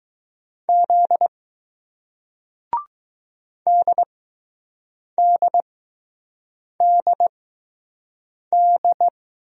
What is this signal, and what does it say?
Morse code, heard as a keyed tone, about 20 wpm, spaced as for 6 wpm, 700 Hz. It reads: Z DDDD